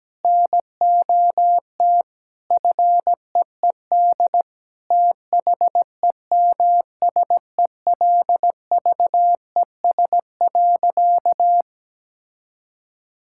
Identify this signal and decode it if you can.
Morse code, heard as a keyed tone, about 17 wpm, 700 Hz. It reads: NOT FEED THEMSELVES.
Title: Sentences from Top 1600 Words 17wpm